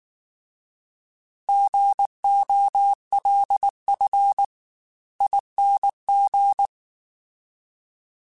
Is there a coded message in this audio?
Yes: Morse code from 1.5 to 6.7 s, heard as a keyed tone, about 19 wpm, spaced as for 16 wpm, 800 Hz. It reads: GOLFING